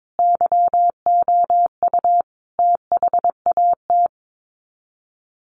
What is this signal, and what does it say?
Morse code, heard as a keyed tone, about 22 wpm, 700 Hz. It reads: YOU THAT
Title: Sets of 5 Words 22wpm